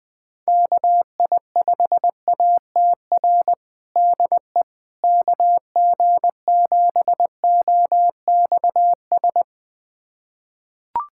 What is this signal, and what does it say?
Morse code, heard as a keyed tone, about 20 wpm, 700 Hz. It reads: KI5ATR DE KG7OXS
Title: USA Callsign DE Callsign 20wpm